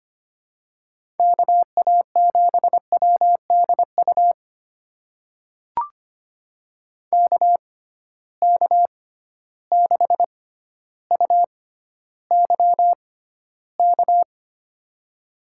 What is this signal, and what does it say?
Morse code, heard as a keyed tone, about 25 wpm, spaced as for 13 wpm, 700 Hz. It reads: KA7WDU KK6UYK
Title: USA Call Signs - 6 Characters - 6x Character Spacing - 25wpm